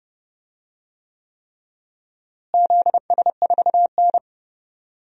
Morse code, heard as a keyed tone, about 30 wpm, 700 Hz. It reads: ZS4N
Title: DXCC-100 Call Signs - 4 Characters - Repeated 3x - 30wpm